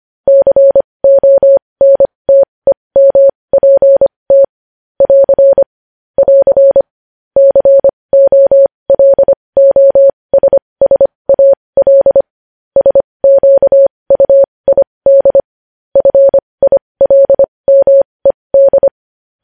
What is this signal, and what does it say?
Morse code, heard as a keyed tone, about 25 wpm, spaced as for 20 wpm, 550 Hz. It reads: CONTEMPT <AR> <AR> COLOSSAL SQUID FILMED
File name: podcast / News headlines-1x-25x20